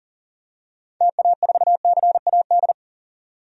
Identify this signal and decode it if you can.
Morse code, heard as a keyed tone, about 40 wpm, 700 Hz. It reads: TA4CAD
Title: DXCC-100 Call Signs - 6 Characters - 40wpm